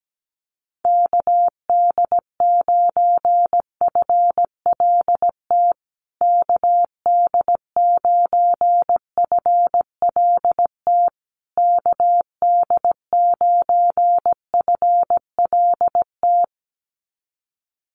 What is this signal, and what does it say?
Morse code, heard as a keyed tone, about 17 wpm, 700 Hz. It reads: KD9FLT KD9FLT KD9FLT
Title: USA Call Signs Repeated 3x 17wpm